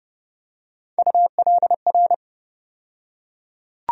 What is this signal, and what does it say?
Morse code, heard as a keyed tone, about 30 wpm, 700 Hz. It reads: ULR
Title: USA Call Sign Postfixes - 3 Letters 30wpm